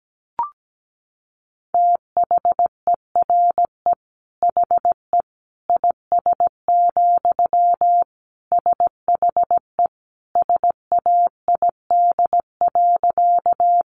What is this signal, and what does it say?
Morse code, heard as a keyed tone, about 17 wpm, 700 Hz. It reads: THERE HE IS, SHE SAID.
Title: Sentences from Top 100 Words 17wpm